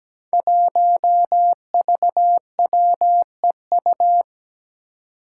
Morse code, heard as a keyed tone, about 17 wpm, 700 Hz. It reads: 1VWEU
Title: Five Challenging Letter-Number Combinations 17wpm